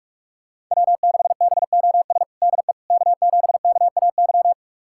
Morse code, heard as a keyed tone, about 41 wpm, 700 Hz. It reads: W6BOS DE K7KAY
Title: USA Callsign DE Callsign - 5 Characters - 45wpm